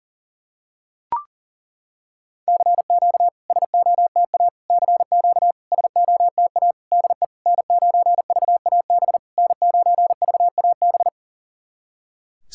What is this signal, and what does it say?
Morse code, heard as a keyed tone, about 37 wpm, 690 Hz. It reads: CQ SOTA CQ SOTA DE N9VAB N9VAB
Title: SOTA QSO Practice 40wpm